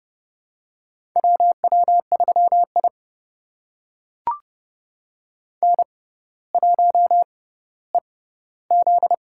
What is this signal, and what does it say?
Morse code, heard as a keyed tone, about 30 wpm, spaced as for 15 wpm, 700 Hz. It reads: WW3I N1EZ
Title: USA Call Signs - 4 Characters - 6x Character Spacing - 30wpm